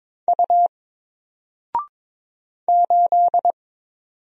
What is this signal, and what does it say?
Morse code, heard as a keyed tone, about 22 wpm, 700 Hz. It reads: U 8
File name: Single-Letter-Number-22wpm